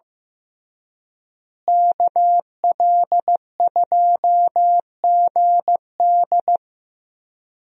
Morse code, heard as a keyed tone, about 15 wpm, 700 Hz. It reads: KL2GD
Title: USA Call Signs - 5 Characters - Repeated 2x - 15wpm